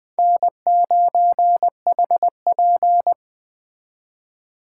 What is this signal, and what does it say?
Morse code, heard as a keyed tone, about 21 wpm, 700 Hz. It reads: N9HP
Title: Speed Racing - USA Call Signs 17wpm